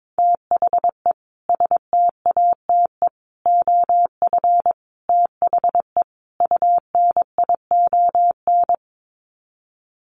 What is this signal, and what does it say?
Morse code, heard as a keyed tone, about 22 wpm, 700 Hz. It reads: THE STATE OF THE UNION
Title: Sets of 5 Words Encore 22wpm